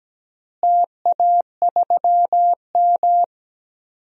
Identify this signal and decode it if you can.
Morse code, heard as a keyed tone, about 17 wpm, 700 Hz. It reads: TA3M
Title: DXCC-100 Call Signs - 4 Characters - Repeated 3x - 17wpm